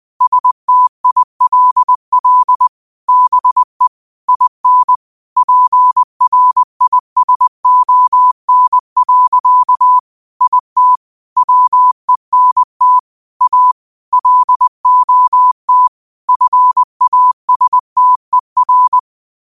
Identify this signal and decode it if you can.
Morse code, heard as a keyed tone, about 20 wpm, 1000 Hz. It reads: STILL BE IN PRISON. IT WENT A LOT FASTER